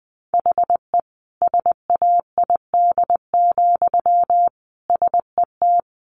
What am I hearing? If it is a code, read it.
Morse code, heard as a keyed tone, about 20 wpm, 700 Hz. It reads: HE SAID, SET
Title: Sentences from Top 1400 Words 20wpm